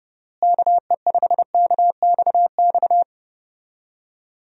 Morse code, heard as a keyed tone, about 30 wpm, 700 Hz. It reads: KE5KXX
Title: USA Call Signs - 6 Characters - Repeated 2x - 30wpm